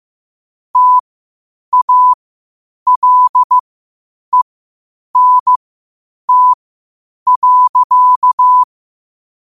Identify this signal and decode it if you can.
Morse code, heard as a keyed tone, about 15 wpm, spaced as for 8 wpm, 1000 Hz. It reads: TALENT.
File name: qotdfiles / 8wpm